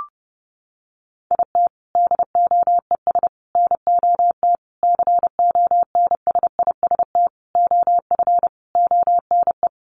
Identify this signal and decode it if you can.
Morse code, heard as a keyed tone, about 30 wpm, 700 Hz. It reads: IT DOES NOT CONSIST OF ONE
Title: Sentences from Top 1400 Words 30wpm